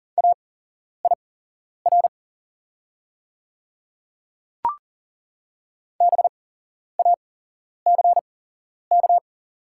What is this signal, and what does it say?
Morse code, heard as a keyed tone, about 40 wpm, spaced as for 11 wpm, 700 Hz. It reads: AIR BACK
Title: 8x Character Spacing - Top 500 Words - 40wpm